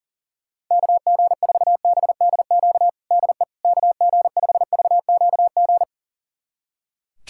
Morse code, heard as a keyed tone, about 40 wpm, 700 Hz. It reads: KG4BDQ DE KG5VQG
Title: USA Callsign DE Callsign 40wpm